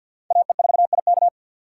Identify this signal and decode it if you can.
Morse code, heard as a keyed tone, about 50 wpm, 700 Hz. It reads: AE4IK
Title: USA Call Signs - 5 Characters - 2x Character Spacing - 50wpm